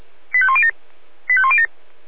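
Five-tone selective call (ZVEI), heard as quick runs of five tones, tones of about 70 ms. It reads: 85107 85107
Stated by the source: ZVEI five-tone selective call 85105 received via SDR.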